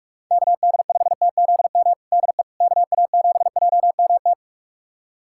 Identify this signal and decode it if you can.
Morse code, heard as a keyed tone, about 46 wpm, 690 Hz. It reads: KD5TZM DE KA7JMT
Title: USA Callsign DE Callsign 45wpm